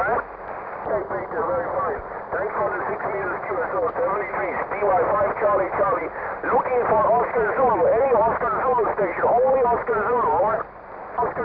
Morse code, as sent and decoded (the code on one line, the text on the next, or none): none